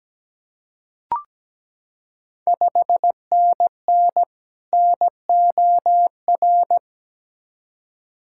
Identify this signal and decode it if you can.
Morse code, heard as a keyed tone, about 17 wpm, 690 Hz. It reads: 5NN NOR
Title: Ontario QSO Party Exchange 17wpm